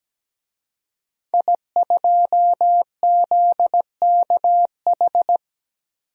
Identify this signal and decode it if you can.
Morse code, heard as a keyed tone, about 17 wpm, 700 Hz. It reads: I2ZKH
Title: DXCC-100 Call Signs - 5 Characters - 17wpm